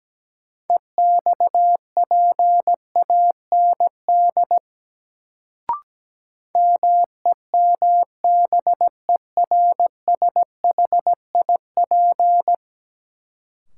Morse code, heard as a keyed tone, about 17 wpm, 700 Hz. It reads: EXPAND MEMBERSHIP
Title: Sentences from Top 1400 Words - New Word Drill - 17wpm